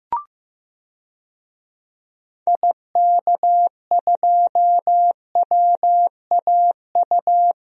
Morse code, heard as a keyed tone, about 15 wpm, 700 Hz. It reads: IK2WAU
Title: DXCC-100 Call Signs - 6 Characters - 15wpm